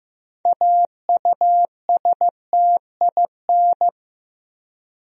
Morse code, heard as a keyed tone, about 15 wpm, 700 Hz. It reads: AUSTIN
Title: USA Common Names - 15wpm